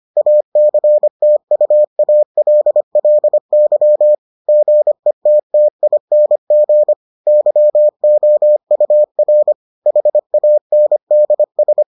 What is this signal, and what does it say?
Morse code, heard as a keyed tone, about 25 wpm, 600 Hz. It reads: ACTUALLY GETTING YOUR HANDS